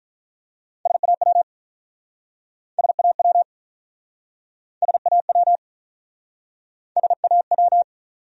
Morse code, heard as a keyed tone, about 47 wpm, 700 Hz. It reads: SAW SAW SAW SAW